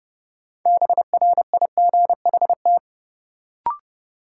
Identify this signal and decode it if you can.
Morse code, heard as a keyed tone, about 32 wpm, 700 Hz. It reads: BRIGHT
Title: Sentences from Top 1000 Words - New Word Drill - 30wpm